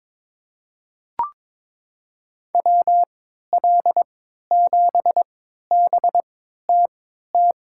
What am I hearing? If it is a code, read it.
Morse code, heard as a keyed tone, about 22 wpm, spaced as for 12 wpm, 700 Hz. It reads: WL7BTT
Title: USA Call Signs - 6 Characters - 3x Character Spacing - 22wpm